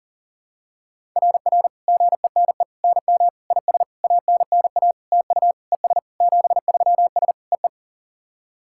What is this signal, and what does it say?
Morse code, heard as a keyed tone, about 44 wpm, 700 Hz. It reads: RR GENE NM IS ANNA TU ES 73S EE
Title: SOTA QSO Practice 40wpm